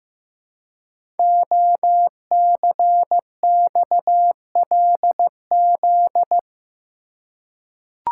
Morse code, heard as a keyed tone, about 15 wpm, 700 Hz. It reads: OCXLZ
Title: Code Groups - 5 Letters - 15wpm